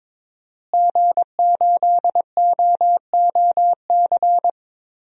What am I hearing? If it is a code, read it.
Morse code, heard as a keyed tone, about 22 wpm, 700 Hz. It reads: G8OOC
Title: DXCC-100 Call Signs - 5 Characters - Repeated 3x - 22wpm